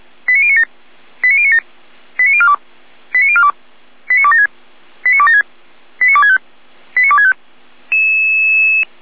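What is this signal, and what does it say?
Five-tone selective call (ZVEI), heard as quick runs of five tones, tones of about 70 ms. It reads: 89097 89097 89042 89042 89276 89276 89265 89265
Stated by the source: ZVEI five-tone selective call 89277, received by SDR.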